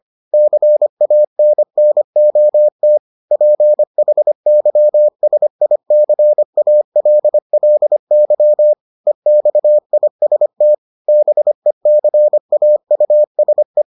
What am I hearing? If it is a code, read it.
Morse code, heard as a keyed tone, about 25 wpm, 600 Hz. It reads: CANNOT PHYSICALLY EXIST BECAUSE